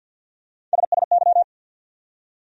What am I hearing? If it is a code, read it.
Morse code, heard as a keyed tone, about 52 wpm, 700 Hz. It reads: SSY
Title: USA Call Sign Postfixes - 3 Letters 50wpm